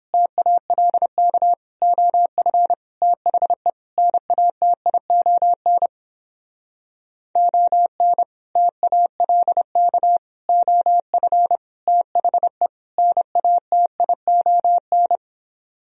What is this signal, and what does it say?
Morse code, heard as a keyed tone, about 29 wpm, 700 Hz. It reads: TALK OF THE NATION ON TALK OF THE NATION